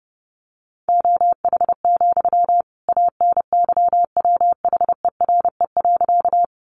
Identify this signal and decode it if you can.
Morse code, heard as a keyed tone, about 30 wpm, 700 Hz. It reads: OH, ANYWHERE.
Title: Sentences from Top 1400 Words 30wpm